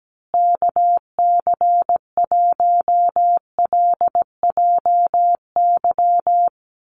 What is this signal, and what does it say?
Morse code, heard as a keyed tone, about 17 wpm, 700 Hz. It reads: KC1LJY
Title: USA Call Signs Repeated 3x 17wpm